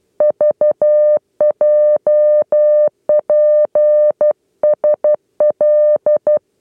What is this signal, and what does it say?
Morse code, heard as a keyed tone, about 12 wpm, 600 Hz. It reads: VJPSL
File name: 8 lezione cw 2